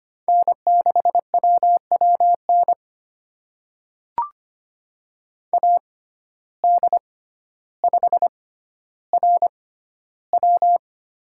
Morse code, heard as a keyed tone, about 25 wpm, spaced as for 13 wpm, 700 Hz. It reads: N6WWN AD5RW
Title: USA Call Signs - 5 Characters - 6x Character Spacing - 25wpm